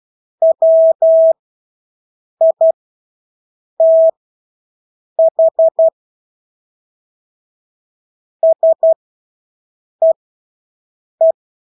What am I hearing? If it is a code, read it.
Morse code, heard as a keyed tone, about 12 wpm, spaced as for 6 wpm, 650 Hz. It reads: WITH SEE
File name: Engelske ord 12-6 wpm